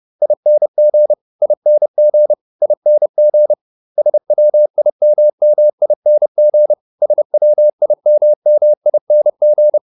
Morse code, heard as a keyed tone, about 30 wpm, 600 Hz. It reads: ING ING ING SWIMMING SWIMMING
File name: ING8 30